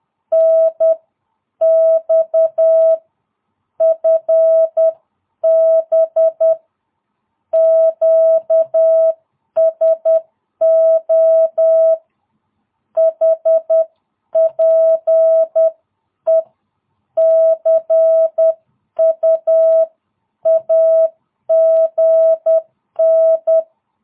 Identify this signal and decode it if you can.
Morse code, heard as a keyed tone, about 10 wpm, spaced as for 9 wpm, 650 Hz. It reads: NX FB QSO HPE CUAGN
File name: jh8cbh / 2022cw